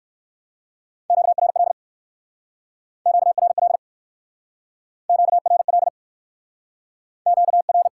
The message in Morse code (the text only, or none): QRL QRL QRL Q R